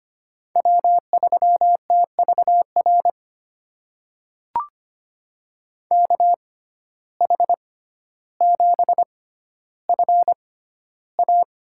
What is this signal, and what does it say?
Morse code, heard as a keyed tone, about 25 wpm, spaced as for 13 wpm, 700 Hz. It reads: W3TVR KH7FA
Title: USA Call Signs - 5 Characters - 6x Character Spacing - 25wpm